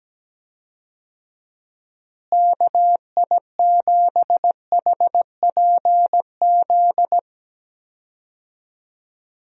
Morse code, heard as a keyed tone, about 17 wpm, 700 Hz. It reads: KI7HPZ
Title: USA Call Signs - 6 Characters - Repeated 2x - 17wpm